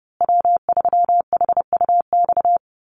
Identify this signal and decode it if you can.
Morse code, heard as a keyed tone, about 30 wpm, 700 Hz. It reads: W3HUX
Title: USA Call Signs Repeated 2x 30wpm